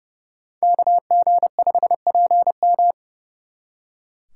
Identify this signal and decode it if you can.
Morse code, heard as a keyed tone, about 38 wpm, 700 Hz. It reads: KG5PM